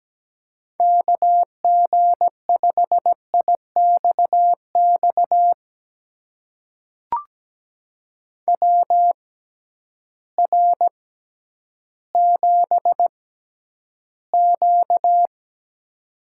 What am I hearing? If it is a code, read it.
Morse code, heard as a keyed tone, about 17 wpm, spaced as for 10 wpm, 700 Hz. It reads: KG5IXX WR7Q
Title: USA Call Signs - 6x Character Spacing - 17wpm